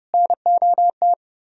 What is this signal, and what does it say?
Morse code, heard as a keyed tone, about 36 wpm, 700 Hz. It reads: NOT
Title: Speed-Racing - Sets of 4 Words 30wpm